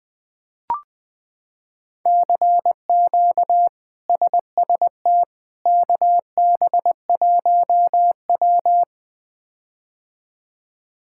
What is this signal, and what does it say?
Morse code, heard as a keyed tone, about 20 wpm, 700 Hz. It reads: CQ SST KB1W
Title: K1USN SST QSO Practice - 20wpm